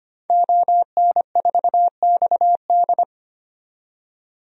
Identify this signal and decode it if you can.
Morse code, heard as a keyed tone, about 25 wpm, 700 Hz. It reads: ON4XD
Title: DXCC-100 Call Signs - 5 Characters - 25wpm